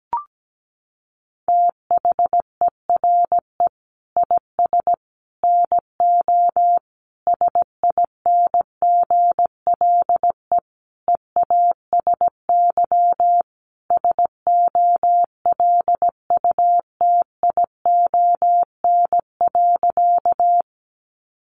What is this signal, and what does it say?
Morse code, heard as a keyed tone, about 17 wpm, 700 Hz. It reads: THERE IS NO SINGLE EASY SOLUTION.
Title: Sentences from Top 1300 Words 17wpm